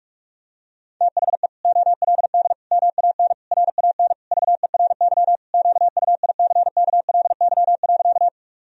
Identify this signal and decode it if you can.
Morse code, heard as a keyed tone, about 46 wpm, 700 Hz. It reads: THE OLD MAN RAN VERY QUICKLY.
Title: Sentences from Top 1000 Words - Review Part 3 - 45wpm